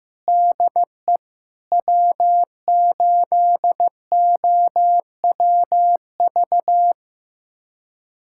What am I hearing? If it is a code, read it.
Morse code, heard as a keyed tone, about 15 wpm, 700 Hz. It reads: DE W8OWV
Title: USA Callsign DE Callsign - 5 Characters - 15wpm